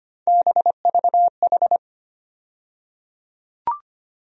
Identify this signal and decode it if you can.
Morse code, heard as a keyed tone, about 25 wpm, 700 Hz. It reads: BVH
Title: USA Call Sign Postfixes - 3 Letters 25wpm